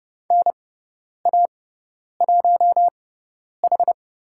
Morse code, heard as a keyed tone, about 30 wpm, spaced as for 10 wpm, 700 Hz. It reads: NA1H
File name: Lesson 17g - Letter u - Call Signs 30wpm